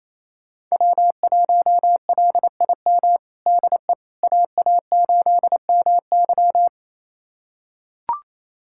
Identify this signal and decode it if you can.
Morse code, heard as a keyed tone, about 28 wpm, 700 Hz. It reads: W1LIM DE AA8MY
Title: USA Callsign DE Callsign - 5 Characters - 28wpm